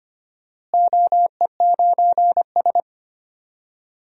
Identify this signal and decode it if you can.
Morse code, heard as a keyed tone, about 25 wpm, 700 Hz. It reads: OE9S
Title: DXCC-100 Call Signs - 4 Characters - Repeated 2x - 25wpm